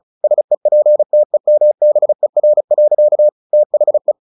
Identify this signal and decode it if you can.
Morse code, heard as a keyed tone, about 34 wpm, 600 Hz. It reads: SEPTEMBER. THE
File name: the old man and the sea / the old man and the sea 1 35wpm